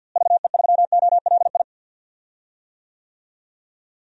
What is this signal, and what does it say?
Morse code, heard as a keyed tone, about 49 wpm, 700 Hz. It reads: VE3OLI
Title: DXCC-100 Call Signs - 6 Characters - Repeated 2x - 50wpm